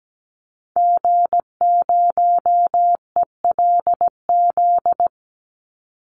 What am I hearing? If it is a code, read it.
Morse code, heard as a keyed tone, about 17 wpm, 700 Hz. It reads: G0ELZ
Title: Call Signs - 17wpm